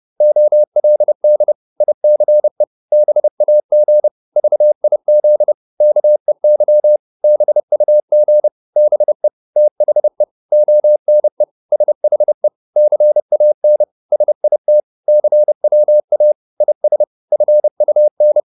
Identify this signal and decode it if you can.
Morse code, heard as a keyed tone, about 30 wpm, 600 Hz. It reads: OLD ICE BAG VIZ KEY BUG BE THE ONE SHE CAN SIT CWA IS FUN
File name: PR204 30